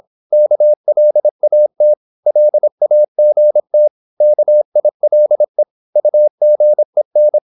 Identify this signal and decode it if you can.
Morse code, heard as a keyed tone, about 26 wpm, 600 Hz. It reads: KLAT LAGT KILE UGEN
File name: Korte ord DK 26wpm